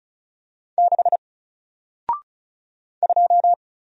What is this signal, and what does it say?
Morse code, heard as a keyed tone, about 33 wpm, 700 Hz. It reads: 6 2
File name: Single-Letter-Number-35wpm